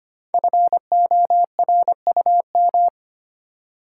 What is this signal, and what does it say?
Morse code, heard as a keyed tone, about 25 wpm, 700 Hz. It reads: FORUM
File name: 5 Letter Words - First 3 Letters Given - 25wpm